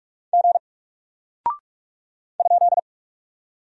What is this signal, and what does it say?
Morse code, heard as a keyed tone, about 46 wpm, 700 Hz. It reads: G ?
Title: Single Character (Letter:Number:Punctuation:Prosign) Alt 45wpm